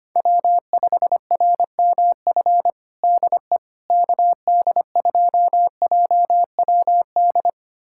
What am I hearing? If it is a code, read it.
Morse code, heard as a keyed tone, about 25 wpm, 700 Hz. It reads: W5RMF DE KD2JWD
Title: USA Callsign DE Callsign 25wpm